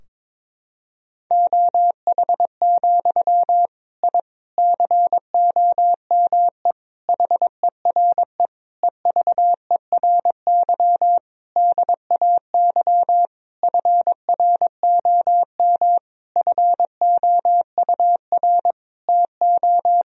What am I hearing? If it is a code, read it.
Morse code, heard as a keyed tone, about 21 wpm, 700 Hz. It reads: OH, I COME HERE EVERY DAY FROM FOUR TO
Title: Sentences from Top 1000 Words - Review Part 2 - 22wpm